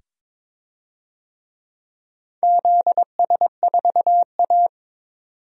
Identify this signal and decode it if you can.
Morse code, heard as a keyed tone, about 22 wpm, 700 Hz. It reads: ZS4A
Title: DXCC-100 Call Signs - 4 Characters - 22wpm